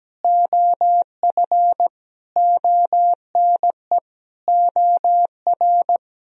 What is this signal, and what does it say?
Morse code, heard as a keyed tone, about 17 wpm, 700 Hz. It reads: OF ONE OR
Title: Sentences from Top 1000 Words - Review Part 4 - 17wpm